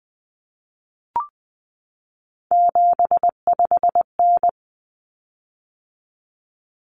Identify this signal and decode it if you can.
Morse code, heard as a keyed tone, about 20 wpm, 700 Hz. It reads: 75N
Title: Common Cut Numbers - 3-Digit 20wpm